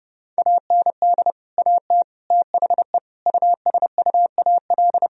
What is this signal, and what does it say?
Morse code, heard as a keyed tone, about 30 wpm, 700 Hz. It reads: AND AT THE USUAL
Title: Sentences from Top 1000 Words - Review Part 2 - 30wpm